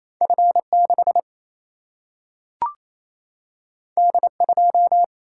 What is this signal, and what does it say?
Morse code, heard as a keyed tone, about 28 wpm, 700 Hz. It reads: F6 D2
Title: Single-Letter Single-Number 28wpm